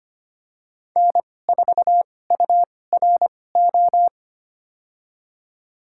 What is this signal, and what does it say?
Morse code, heard as a keyed tone, about 25 wpm, spaced as for 18 wpm, 700 Hz. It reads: N4URO
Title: USA Call Signs - 5 Characters - 2x Character Spacing - 25wpm